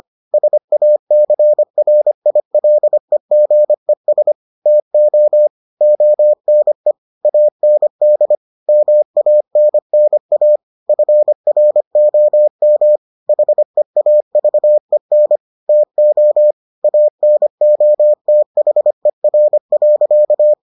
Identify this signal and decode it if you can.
Morse code, heard as a keyed tone, about 25 wpm, 600 Hz. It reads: SACRILEGES TO ONE AND MANNA FROM HEAVEN TO ANOTHER.